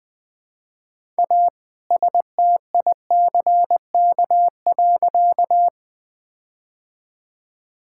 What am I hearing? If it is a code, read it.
Morse code, heard as a keyed tone, about 21 wpm, 700 Hz. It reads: A STICK.
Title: All-Age Appropriate Jokes - 20wpm